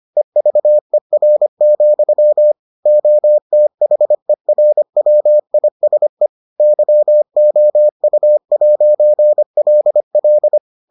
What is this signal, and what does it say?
Morse code, heard as a keyed tone, about 25 wpm, 600 Hz. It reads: EVER, OTHERWISE YOU'LL